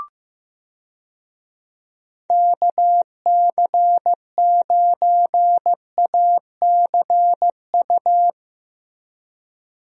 Morse code, heard as a keyed tone, about 15 wpm, 700 Hz. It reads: KC9ACU
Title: USA Call Signs - 6 Characters - Repeated 3x - 15wpm